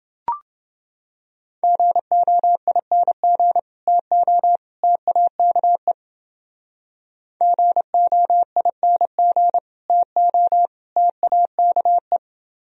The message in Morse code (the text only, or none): GOING TO TAKE GOING TO TAKE